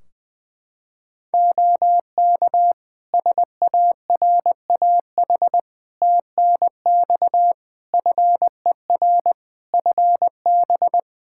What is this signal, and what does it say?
Morse code, heard as a keyed tone, about 20 wpm, 700 Hz. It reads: OK SARAH TNX FER FB
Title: Practice QSO - Exchange 4 - Ending - 20wpm